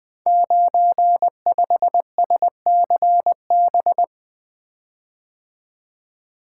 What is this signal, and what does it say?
Morse code, heard as a keyed tone, about 20 wpm, 700 Hz. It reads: 95SCB
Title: Code Groups - 5 Letter:Numbers - 20wpm